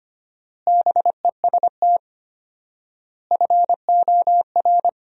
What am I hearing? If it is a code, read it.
Morse code, heard as a keyed tone, about 25 wpm, 700 Hz. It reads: BEST FOR
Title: 4x Word-Spacing - Sentences from Top 500 Words - 25wpm